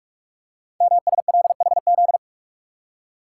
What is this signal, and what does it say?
Morse code, heard as a keyed tone, about 44 wpm, 700 Hz. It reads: MSLHZ
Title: Code Groups - 5 Letter:Numbers - 45wpm